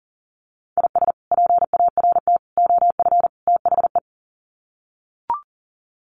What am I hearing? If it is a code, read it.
Morse code, heard as a keyed tone, about 37 wpm, 700 Hz. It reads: IS PART OF THE
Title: Sets of 4 Words 40wpm